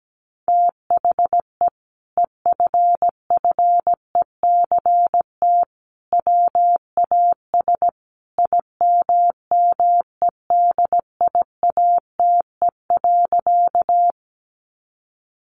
Morse code, heard as a keyed tone, about 17 wpm, 700 Hz. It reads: THE EFFECT WAS IMMEDIATE.
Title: Sentences from Top 1200 Words 17wpm